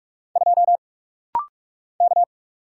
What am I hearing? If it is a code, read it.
Morse code, heard as a keyed tone, about 46 wpm, 700 Hz. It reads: 2 K
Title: Single Letter-Number - Rapid-Fire 45wpm